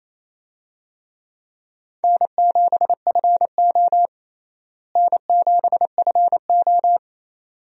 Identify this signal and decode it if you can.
Morse code, heard as a keyed tone, about 28 wpm, 700 Hz. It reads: N7FO N7FO
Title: USA Call Signs - 4 Characters - Repeated 2x - 28wpm